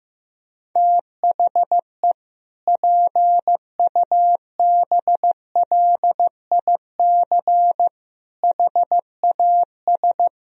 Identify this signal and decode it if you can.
Morse code, heard as a keyed tone, about 15 wpm, 700 Hz. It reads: THE PUBLIC HAS
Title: Sentences from Top 1000 Words - Review Part 3 - 15wpm